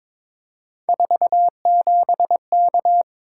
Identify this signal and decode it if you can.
Morse code, heard as a keyed tone, about 23 wpm, 700 Hz. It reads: 47K
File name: practice / 22wpm / Three Letter-Number Combinations - 22wpm